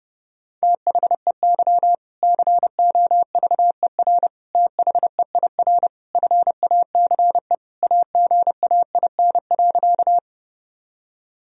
Morse code, heard as a keyed tone, about 30 wpm, 700 Hz. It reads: THEY COVER THEIR FACE AGAIN.
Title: Sentences from Top 1000 Words - Review Part 1 - 30wpm